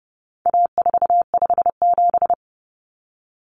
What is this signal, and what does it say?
Morse code, heard as a keyed tone, about 30 wpm, 700 Hz. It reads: A457
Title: Common Cut Numbers - 4-Digit 30wpm